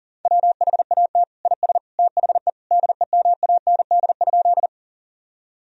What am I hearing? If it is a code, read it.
Morse code, heard as a keyed tone, about 37 wpm, 700 Hz. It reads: WHAT IS THE DEMAND?
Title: Sentences from Top 1000 Words - Review Part 4 - 40wpm